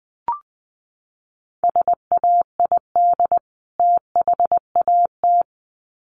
Morse code, heard as a keyed tone, about 20 wpm, 700 Hz. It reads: SAID THAT
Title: Sets of 2 Words 20wpm